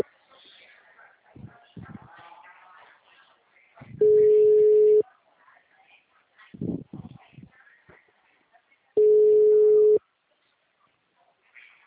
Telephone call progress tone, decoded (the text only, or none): ringback tone